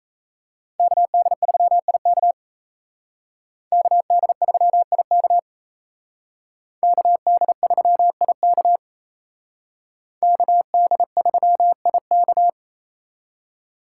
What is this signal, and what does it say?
Morse code, heard as a keyed tone, about 32 wpm, 700 Hz. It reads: KD3IK KD3IK KD3IK KD3IK
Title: Speed Racing - USA 2x2 Call Signs 28wpm